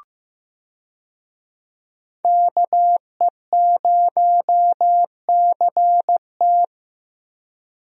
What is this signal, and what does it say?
Morse code, heard as a keyed tone, about 15 wpm, 700 Hz. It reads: KE0CT
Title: USA Call Signs - 5 Characters - Repeated 3x - 15wpm